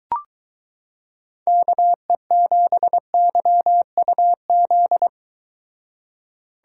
Morse code, heard as a keyed tone, about 23 wpm, 700 Hz. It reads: KE7YUZ